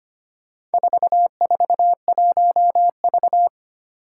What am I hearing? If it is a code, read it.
Morse code, heard as a keyed tone, about 25 wpm, 700 Hz. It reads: 441V
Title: Four Challenging Letter-Number Combinations 25wpm